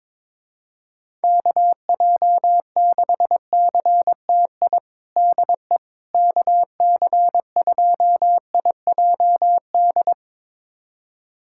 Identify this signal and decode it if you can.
Morse code, heard as a keyed tone, about 22 wpm, 700 Hz. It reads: KJ6CTI DE KC2IJD
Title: USA Callsign DE Callsign - 6 Characters - 22wpm